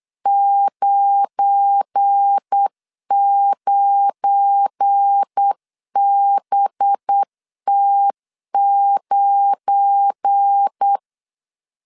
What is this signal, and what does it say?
Morse code, heard as a keyed tone, about 8 wpm, 800 Hz. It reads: 99BT9
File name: LEZIONI / 7 GF9Z / 7 GF9 8wpm 3545